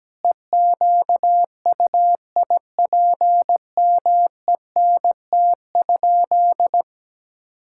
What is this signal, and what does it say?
Morse code, heard as a keyed tone, about 17 wpm, 700 Hz. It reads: EQUIPMENT?
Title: Sentences from Top 1000 Words - Review Part 5 - 17wpm